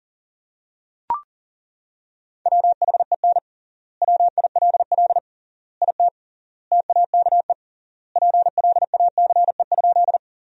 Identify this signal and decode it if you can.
Morse code, heard as a keyed tone, about 40 wpm, 700 Hz. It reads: WHEN WILL IT TAKE PLACE?
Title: 3x Word-Spacing - Sentences from Top 200 Words - 40wpm